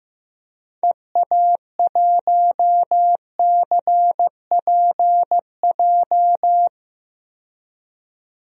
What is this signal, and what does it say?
Morse code, heard as a keyed tone, about 16 wpm, 700 Hz. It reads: EA1CPJ
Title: DXCC-100 Call Signs - 6 Characters - 15wpm